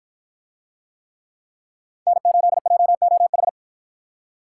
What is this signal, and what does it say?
Morse code, heard as a keyed tone, about 48 wpm, 700 Hz. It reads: N8JOH